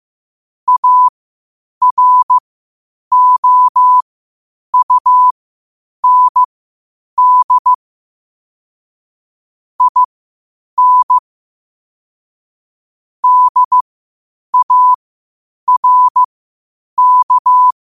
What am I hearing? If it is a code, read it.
Morse code, heard as a keyed tone, about 15 wpm, spaced as for 8 wpm, 1000 Hz. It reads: AROUND IN DARK
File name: qotdfiles / 8wpm